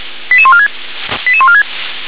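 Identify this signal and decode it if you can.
Five-tone selective call (ZVEI), heard as quick runs of five tones, tones of about 70 ms. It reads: 88146 88146
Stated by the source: ZVEI five-tone selective call 88152, received on an SDR.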